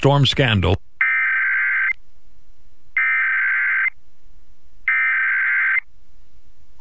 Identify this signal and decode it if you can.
Emergency Alert System (SAME) tones (header bursts), heard as harsh buzzing bursts, with speech. It is heard from 1.0 to 5.8 s.
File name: WCCO / WCCO-AM EAS Test 11-9-11 1PM